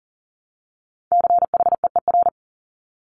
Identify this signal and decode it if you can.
Morse code, heard as a keyed tone, about 38 wpm, 700 Hz. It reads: CHEER
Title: Multi-letter Phonemes Repeated 3x 40wpm